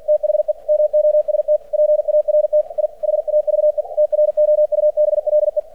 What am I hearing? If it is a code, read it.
Morse code, heard as a keyed tone, about 48 wpm, 600 Hz. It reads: THE MOST PART I HAVE TROUBLE
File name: archived / CRSnet